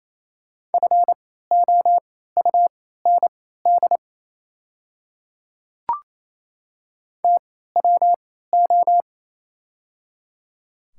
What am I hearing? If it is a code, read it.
Morse code, heard as a keyed tone, about 28 wpm, spaced as for 16 wpm, 700 Hz. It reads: FOUND TWO
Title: 3x Character Spacing - Top 300 Words - 28wpm